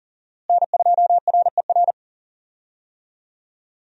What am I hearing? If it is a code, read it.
Morse code, heard as a keyed tone, about 40 wpm, 700 Hz. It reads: N2RER
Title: USA Call Signs - 5 Characters - Repeated 2x - 40wpm